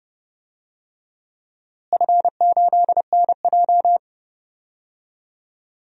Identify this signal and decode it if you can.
Morse code, heard as a keyed tone, about 29 wpm, 700 Hz. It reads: F8NJ
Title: DXCC-100 Call Signs - 4 Characters - Repeated 3x - 30wpm